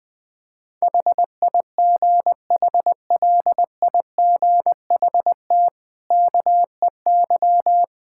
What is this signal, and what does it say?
Morse code, heard as a keyed tone, about 21 wpm, 700 Hz. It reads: HIGHLIGHT KEY
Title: Easy Sentences with Hints - 20wpm